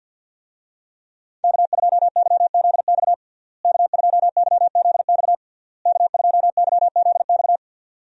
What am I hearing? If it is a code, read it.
Morse code, heard as a keyed tone, about 51 wpm, 700 Hz. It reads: K2YZX K2YZX K2YZX
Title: USA Call Signs - 5 Characters - Repeated 3x - 50wpm